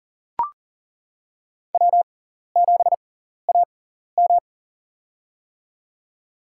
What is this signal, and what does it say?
Morse code, heard as a keyed tone, about 40 wpm, spaced as for 14 wpm, 700 Hz. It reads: W7AM
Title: USA Call Signs - 4 Characters - 6x Character Spacing - 40wpm